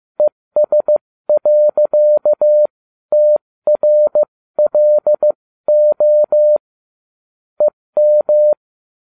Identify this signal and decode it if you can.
Morse code, heard as a keyed tone, about 15 wpm, spaced as for 13 wpm, 600 Hz. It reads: ES.TRLO EM